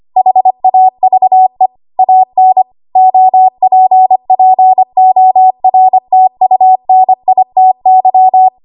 Morse code, heard as a keyed tone, about 25 wpm, 750 Hz. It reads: HAVE AN OPPORTUNITY
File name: Archive / 25 WPM / 25WPM